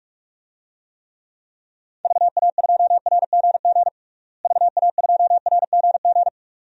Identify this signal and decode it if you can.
Morse code, heard as a keyed tone, about 43 wpm, 700 Hz. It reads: VA2RGG VA2RGG
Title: DXCC-100 Call Signs - 6 Characters - Repeated 2x - 45wpm